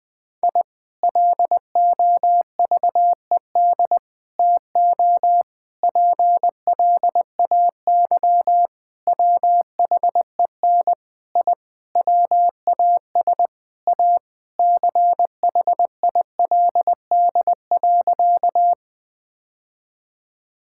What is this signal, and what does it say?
Morse code, heard as a keyed tone, about 21 wpm, 700 Hz. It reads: I LOVED TO PLAY WHEN I WAS A CHILD.
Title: Sentences with 4-Word Phrases - 20wpm